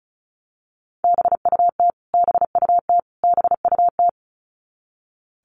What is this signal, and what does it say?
Morse code, heard as a keyed tone, about 34 wpm, 700 Hz. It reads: BUT BUT BUT
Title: Top 300 Words Repeated 35wpm